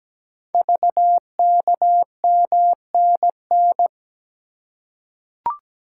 Morse code, heard as a keyed tone, about 17 wpm, 700 Hz. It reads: VKMNN
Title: Code Groups - 5 Letters - 17wpm